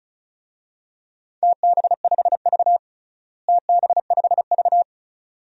Morse code, heard as a keyed tone, about 36 wpm, 700 Hz. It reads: TB5V TB5V
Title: DXCC-100 Call Signs - 4 Characters - Repeated 2x - 35wpm